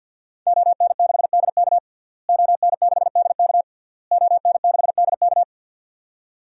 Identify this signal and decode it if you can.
Morse code, heard as a keyed tone, about 52 wpm, 700 Hz. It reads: ON6DK ON6DK ON6DK
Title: DXCC-100 Call Signs - 5 Characters - Repeated 3x - 50wpm